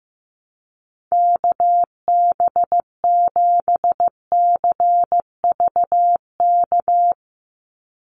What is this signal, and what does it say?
Morse code, heard as a keyed tone, about 15 wpm, 700 Hz. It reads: KB7CVK
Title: USA Call Signs 15wpm